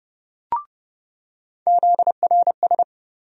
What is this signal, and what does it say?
Morse code, heard as a keyed tone, about 30 wpm, 700 Hz. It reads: ZRS
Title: USA Call Sign Postfixes - 3 Letters 30wpm